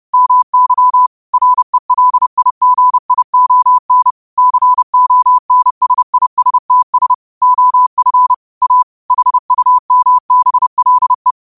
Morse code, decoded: MY RELIGION CONSISTS OF A HUMBLE

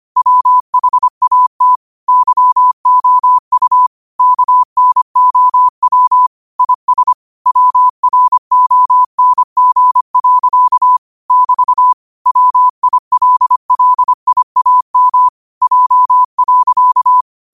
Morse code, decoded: WHAT YOU KNOW IS WRONG. <BT> WILLIAM J.

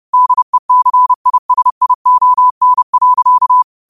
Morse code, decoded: DECISION.

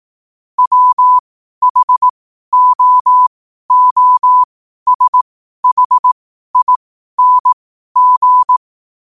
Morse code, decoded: WHOOSHING

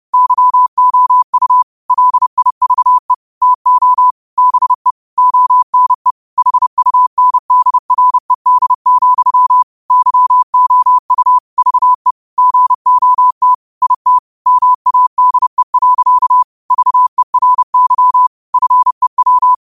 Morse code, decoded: YOU LIVE TO BE ONE HUNDRED, YOU VE GOT IT MADE. VERY FEW

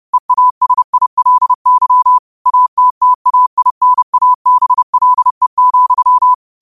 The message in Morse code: EASILY ATTAINABLE,

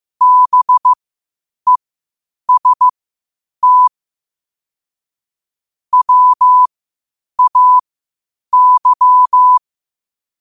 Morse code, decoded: BEST WAY

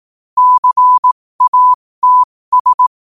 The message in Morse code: CATS

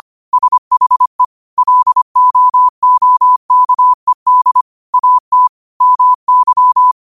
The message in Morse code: SHE LOOKED AT MY

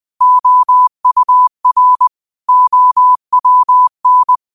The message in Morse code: OUR OWN

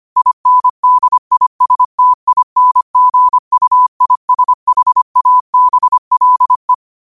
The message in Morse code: INDISTINGUISHABLE

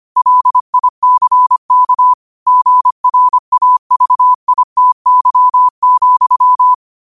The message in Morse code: LICK GRAVITY,

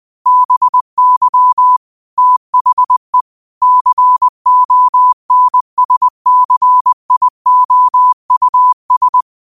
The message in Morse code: BY THE CONSCIOUS

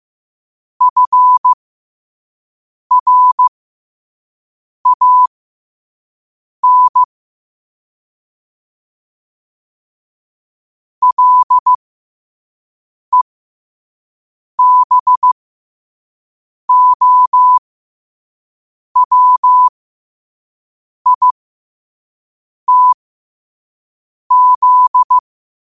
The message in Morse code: FRAN LEBOWITZ

